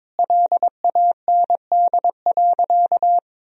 Morse code, heard as a keyed tone, about 22 wpm, 700 Hz. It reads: LAND.
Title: Sentences from Top 1000 Words - Review Part 3 - 22wpm